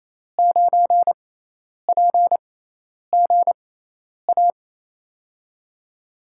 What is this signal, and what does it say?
Morse code, heard as a keyed tone, about 28 wpm, spaced as for 10 wpm, 700 Hz. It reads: 9PGA
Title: USA Call Signs - 5 Characters - 6x Character Spacing - 28wpm